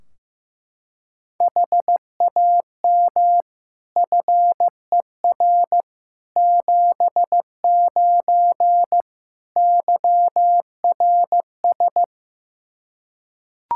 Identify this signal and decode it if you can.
Morse code, heard as a keyed tone, about 15 wpm, 700 Hz. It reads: HAM FER 79 YRS
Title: Practice QSO - Exchange 3 - More Info - 15wpm